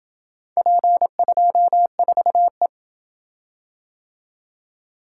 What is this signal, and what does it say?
Morse code, heard as a keyed tone, about 28 wpm, 700 Hz. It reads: P24E